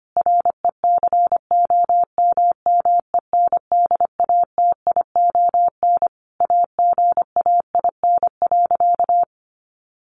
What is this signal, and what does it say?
Morse code, heard as a keyed tone, about 25 wpm, 700 Hz. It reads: RECOMMENDATION AGAIN.
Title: Sentences from Top 1400 Words 25wpm